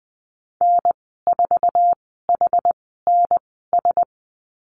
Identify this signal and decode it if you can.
Morse code, heard as a keyed tone, about 20 wpm, spaced as for 15 wpm, 700 Hz. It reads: N4HNS
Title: 069 Lesson 12g - Letter d - Call Signs 20wpm